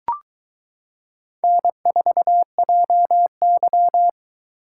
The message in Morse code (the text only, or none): N4JY